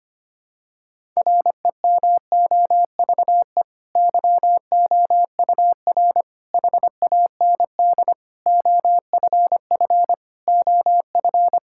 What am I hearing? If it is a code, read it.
Morse code, heard as a keyed tone, about 25 wpm, 700 Hz. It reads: REMOVE YOUR HAND OFF OF
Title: Sentences from Top 1000 Words - Review Part 5 - 25wpm